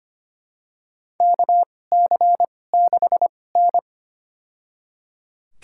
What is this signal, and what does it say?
Morse code, heard as a keyed tone, about 25 wpm, spaced as for 18 wpm, 700 Hz. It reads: KC6N
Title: USA Call Signs - 2x Character Spacing - 25wpm